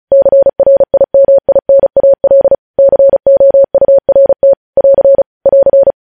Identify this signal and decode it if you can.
Morse code, heard as a keyed tone, about 36 wpm, 550 Hz. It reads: CRIMINAL COURT <AR> <AR>
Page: news headlines in morse code at 35 wpm